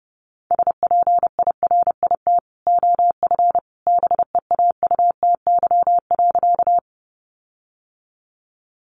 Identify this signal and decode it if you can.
Morse code, heard as a keyed tone, about 28 wpm, 700 Hz. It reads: SPIRIT OF BEAUTY.
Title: Sentences from Top 1400 Words 30wpm